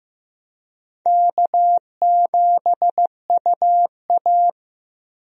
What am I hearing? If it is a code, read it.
Morse code, heard as a keyed tone, about 15 wpm, 700 Hz. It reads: K7UA
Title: USA Call Signs - 4 Characters - Repeated 3x - 15wpm